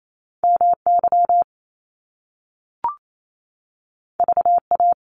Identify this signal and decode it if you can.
Morse code, heard as a keyed tone, about 28 wpm, 700 Hz. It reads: MY VA
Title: Top Two Letter Combinations in Words 28wpm